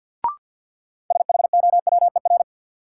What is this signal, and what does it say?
Morse code, heard as a keyed tone, about 52 wpm, 700 Hz. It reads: SHOWER
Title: Sentences from Top 1600 Words - New Word Drill - 50wpm